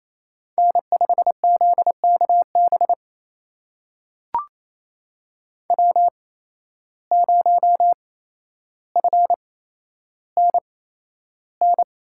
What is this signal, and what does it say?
Morse code, heard as a keyed tone, about 28 wpm, spaced as for 12 wpm, 700 Hz. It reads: N5ZKB W0FNN
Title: USA Call Signs - 5 Characters - 8x Character Spacing - 28wpm